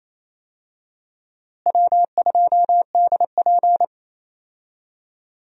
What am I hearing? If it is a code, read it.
Morse code, heard as a keyed tone, about 28 wpm, 700 Hz. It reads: W2DP
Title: USA Call Signs - 4 Characters - Repeated 3x - 28wpm